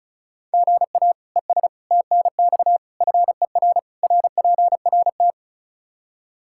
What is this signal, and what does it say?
Morse code, heard as a keyed tone, about 36 wpm, 700 Hz. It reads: GA ES TNX FER RPRT
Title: K1USN SST QSO Practice 35wpm